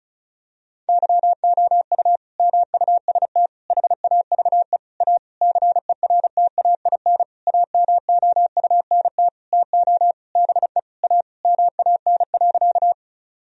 Morse code, heard as a keyed tone, about 34 wpm, 700 Hz. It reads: YOU MUST HAVE A CERTAIN AMOUNT TO BE A MAN.
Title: Sentences from Top 1000 Words - Review Part 4 - 35wpm